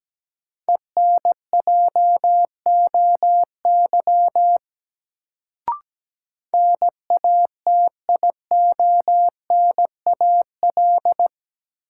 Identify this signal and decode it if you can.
Morse code, heard as a keyed tone, about 17 wpm, 700 Hz. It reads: ENJOY NATIONAL
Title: Sentences from Top 1000 Words - Review Part 4 - New Word Drill - 17wpm